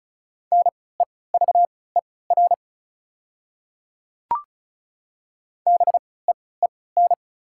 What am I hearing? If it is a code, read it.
Morse code, heard as a keyed tone, about 34 wpm, spaced as for 20 wpm, 690 Hz. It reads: NEVER BEEN
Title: 3x Character Spacing - Top 300 Words - 35wpm